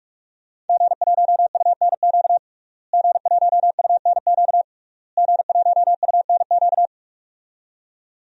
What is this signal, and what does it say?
Morse code, heard as a keyed tone, about 42 wpm, 700 Hz. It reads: G1UNQ G1UNQ G1UNQ
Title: DXCC-100 Call Signs - 5 Characters - Repeated 3x - 45wpm